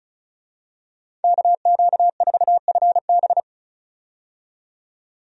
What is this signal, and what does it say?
Morse code, heard as a keyed tone, about 36 wpm, 700 Hz. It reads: KQ4FB
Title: USA Call Signs - 5 Characters - 35wpm